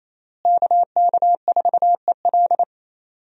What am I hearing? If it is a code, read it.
Morse code, heard as a keyed tone, about 28 wpm, 700 Hz. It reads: KK4EL
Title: USA Call Signs - 5 Characters - Repeated 3x - 28wpm